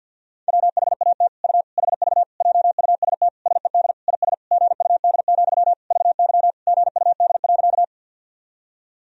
Morse code, heard as a keyed tone, about 50 wpm, 700 Hz. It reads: WHAT U HV JUST SED IS GUD, VY GUD.
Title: Sentences from Top 200 Words and Abbreviations 50wpm